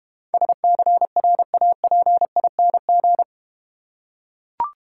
Morse code, heard as a keyed tone, about 31 wpm, 700 Hz. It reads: SCRAPING
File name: Easy Words - Speed-Racing with Hints - 25wpm